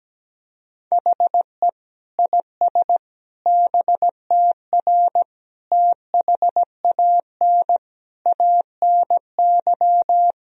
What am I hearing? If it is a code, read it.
Morse code, heard as a keyed tone, about 17 wpm, 700 Hz. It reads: HE IS BTR THAN ANY
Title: Sentences from Top 400 Words and Abbreviations 17wpm